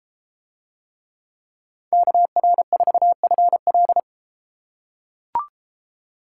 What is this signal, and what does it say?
Morse code, heard as a keyed tone, about 30 wpm, 700 Hz. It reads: KR4FL